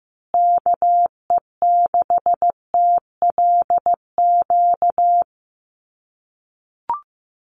Morse code, heard as a keyed tone, about 15 wpm, 700 Hz. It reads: KE6TLQ
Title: USA Call Signs Repeated 2x 15wpm